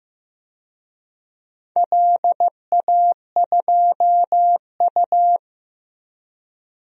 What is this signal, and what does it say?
Morse code, heard as a keyed tone, about 15 wpm, 700 Hz. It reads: LA2U
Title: DXCC-100 Call Signs - 4 Characters - Repeated 2x - 15wpm